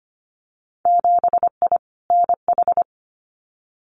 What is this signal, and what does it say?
Morse code, heard as a keyed tone, about 25 wpm, 700 Hz. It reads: 7I NH
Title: Winter Field Day Exchanges Encore 25wpm